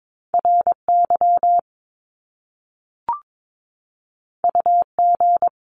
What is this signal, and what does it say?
Morse code, heard as a keyed tone, about 22 wpm, 700 Hz. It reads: RY UG
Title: Top Two Letter Combinations in Words 22wpm